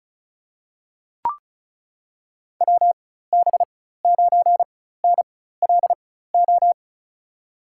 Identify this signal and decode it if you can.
Morse code, heard as a keyed tone, about 34 wpm, spaced as for 16 wpm, 700 Hz. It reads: WB9NLO
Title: USA Call Signs - 4x Character Spacing - 35wpm